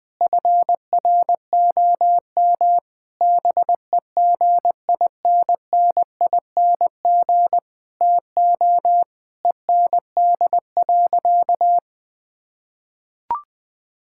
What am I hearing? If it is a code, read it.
Morse code, heard as a keyed tone, about 21 wpm, 700 Hz. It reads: FROM BEGINNING TO END.
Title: Sentences from Top 1000 Words - Review Part 5 - 20wpm